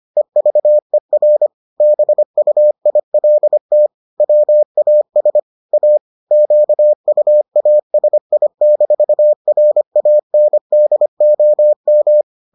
Morse code, heard as a keyed tone, about 25 wpm, 600 Hz. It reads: EVER BUILT WAS A QUASI-RANDOM